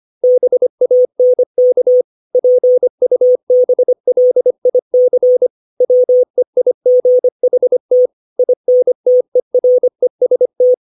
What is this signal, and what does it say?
Morse code, heard as a keyed tone, about 25 wpm, 500 Hz. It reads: BANK PUBLIC WEIGHT INTEREST
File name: 632 words / 632 Words 25WPM